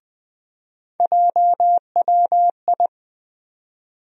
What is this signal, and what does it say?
Morse code, heard as a keyed tone, about 20 wpm, 700 Hz. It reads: JWI
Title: USA Call Sign Postfixes - 3 Letters 20wpm